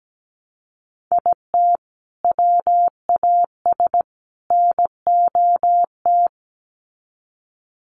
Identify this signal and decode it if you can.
Morse code, heard as a keyed tone, about 17 wpm, 700 Hz. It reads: IT WAS NOT
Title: Sets of 3 Words 17wpm